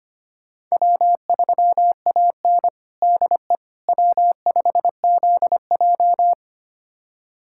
Morse code, heard as a keyed tone, about 25 wpm, 700 Hz. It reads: W3AN DE W5ZJ
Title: USA Callsign DE Callsign - 4 Characters - 25wpm